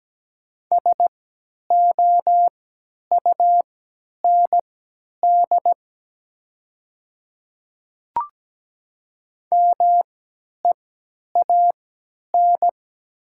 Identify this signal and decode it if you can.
Morse code, heard as a keyed tone, about 17 wpm, spaced as for 10 wpm, 700 Hz. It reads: SOUND MEAN
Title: 3x Character Spacing - Top 300 Words - 17wpm